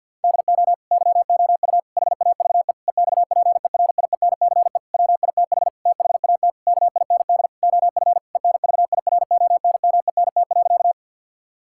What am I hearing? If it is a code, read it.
Morse code, heard as a keyed tone, about 49 wpm, 690 Hz. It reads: DO YOU HAVE EXPERIENCE WITH THAT KIND OF ENVIRONMENT.
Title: Sentences from Top 1000 Words - Review Part 5 - 50wpm